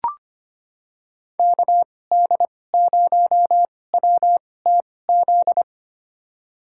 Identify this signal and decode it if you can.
Morse code, heard as a keyed tone, about 25 wpm, spaced as for 18 wpm, 700 Hz. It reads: KD0WTZ
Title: USA Call Signs - 6 Characters - 2x Character Spacing - 25wpm